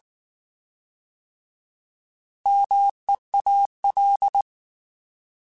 Morse code, heard as a keyed tone, about 19 wpm, 800 Hz. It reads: MEAL
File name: Alphabetical / Alphabetical mp3s / Ms @18